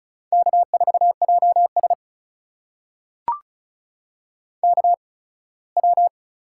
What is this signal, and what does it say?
Morse code, heard as a keyed tone, about 36 wpm, spaced as for 21 wpm, 700 Hz. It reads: K4JS KW
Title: USA Call Signs - 4 Characters - 8x Character Spacing - 35wpm